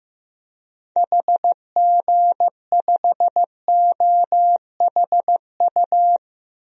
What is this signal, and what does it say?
Morse code, heard as a keyed tone, about 15 wpm, 700 Hz. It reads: HG5OHU
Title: DXCC-100 Call Signs - 6 Characters - Repeated 3x - 15wpm